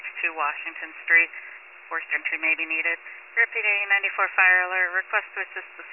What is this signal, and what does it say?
Emergency Alert System (SAME) tones (header bursts), heard as harsh buzzing bursts, with speech. It is heard from 1.0 to 5.3 s.
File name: wcalert